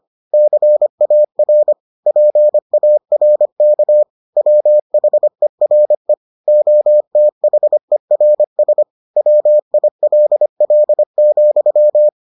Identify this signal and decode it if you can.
Morse code, heard as a keyed tone, about 25 wpm, 600 Hz. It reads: CAR PARK WHERE OTHERS WILL,